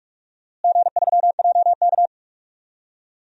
Morse code, heard as a keyed tone, about 42 wpm, 700 Hz. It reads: G3JK
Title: DXCC-100 Call Signs - 4 Characters - Repeated 2x - 45wpm